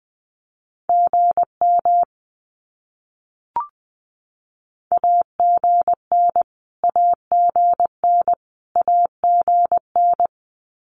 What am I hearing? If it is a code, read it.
Morse code, heard as a keyed tone, about 20 wpm, 700 Hz. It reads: GM AGN AGN AGN
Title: Abbreviations Repeated 3x - 20wpm